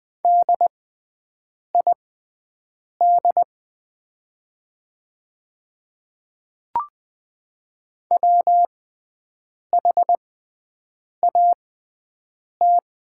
Morse code, heard as a keyed tone, about 20 wpm, spaced as for 7 wpm, 700 Hz. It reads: DID WHAT
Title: 6x Character Spacing - Top 100 Words - 20wpm